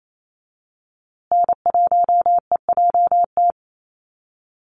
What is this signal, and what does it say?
Morse code, heard as a keyed tone, about 28 wpm, 700 Hz. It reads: N1EJT
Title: USA Call Signs Repeated 2x 28wpm